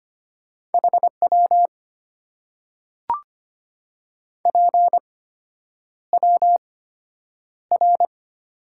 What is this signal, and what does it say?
Morse code, heard as a keyed tone, about 25 wpm, spaced as for 9 wpm, 700 Hz. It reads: HW PWR
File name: Top 90 QSO Elements - 8x Character Spacing - 25wpm